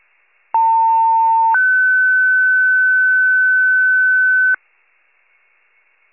Two-tone page, heard as a one-second tone, then a three-second tone, about 900 then 1500 Hz. It is heard from 0.5 to 4.6 s.
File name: wcalert